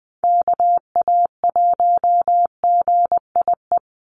Morse code, heard as a keyed tone, about 20 wpm, 700 Hz. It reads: KA1GIE
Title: USA Call Signs 20wpm